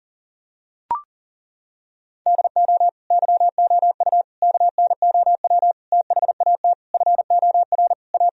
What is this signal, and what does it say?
Morse code, heard as a keyed tone, about 40 wpm, 700 Hz. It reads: DO YOU KNOW THAT FOR A
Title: Sentences from Top 1000 Words - Review Part 3 - 40wpm